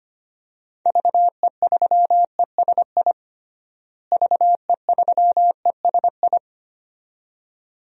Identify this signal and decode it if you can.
Morse code, heard as a keyed tone, about 25 wpm, 700 Hz. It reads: VE3ESI VE3ESI
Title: DXCC-100 Call Signs - 6 Characters - Repeated 2x - 25wpm